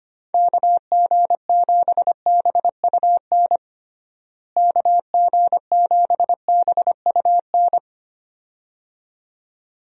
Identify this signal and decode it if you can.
Morse code, heard as a keyed tone, about 25 wpm, 700 Hz. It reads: KG7BUN KG7BUN
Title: USA Call Signs - 6 Characters - Repeated 2x - 25wpm